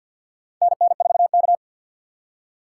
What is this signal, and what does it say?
Morse code, heard as a keyed tone, about 49 wpm, 700 Hz. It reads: NN4K
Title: USA Call Signs - 4 Characters - 3x Character Spacing - 50wpm